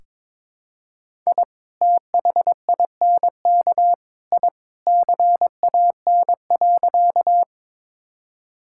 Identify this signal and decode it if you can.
Morse code, heard as a keyed tone, about 22 wpm, 700 Hz. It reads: I THINK I CAN.
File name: practice / 22wpm / Sentences with 3-Word Phrases - 22wpm